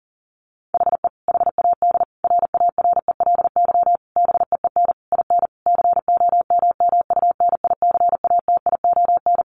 Morse code, heard as a keyed tone, about 40 wpm, 700 Hz. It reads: HE HAD RARELY BEEN IN COMMUNICATION
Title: Sentences from Top 1400 Words 40wpm